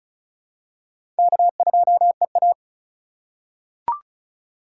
Morse code, heard as a keyed tone, about 34 wpm, 700 Hz. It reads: K2EA
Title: USA Call Signs - 4 Characters - 2x Character Spacing - 35wpm